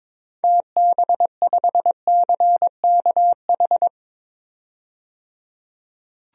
Morse code, heard as a keyed tone, about 21 wpm, 700 Hz. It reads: TB5CKH
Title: DXCC-100 Call Signs - 6 Characters - 22wpm